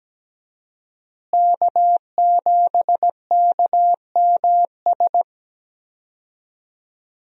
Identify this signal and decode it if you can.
Morse code, heard as a keyed tone, about 17 wpm, 700 Hz. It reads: K7KMS
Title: USA Call Signs - 5 Characters - Repeated 3x - 17wpm